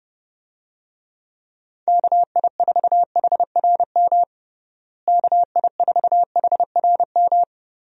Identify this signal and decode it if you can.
Morse code, heard as a keyed tone, about 30 wpm, 700 Hz. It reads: KI4HRM KI4HRM
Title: USA Call Signs - 6 Characters - Repeated 3x - 30wpm